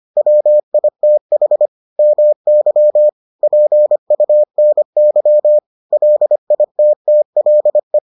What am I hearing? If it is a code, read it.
Morse code, heard as a keyed tone, about 25 wpm, 600 Hz. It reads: WITH MY PUNY LITTLE